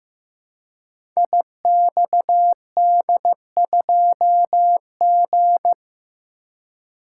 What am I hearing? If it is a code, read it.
Morse code, heard as a keyed tone, about 15 wpm, 700 Hz. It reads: IXD2G
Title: Code Groups - 5 Letter:Numbers - 15wpm